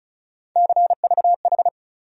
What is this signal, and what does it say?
Morse code, heard as a keyed tone, about 34 wpm, 700 Hz. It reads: CVH
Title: USA Call Sign Postfixes - 3 Letters 35wpm